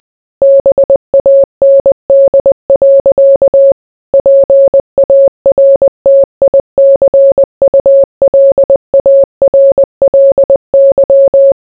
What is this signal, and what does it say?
Morse code, heard as a keyed tone, about 20 wpm, 550 Hz. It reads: BAND. PARTICULARLY